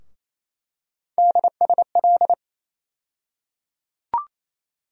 Morse code, heard as a keyed tone, about 28 wpm, 700 Hz. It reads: DSL
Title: USA Call Sign Postfixes - 3 Letters 28wpm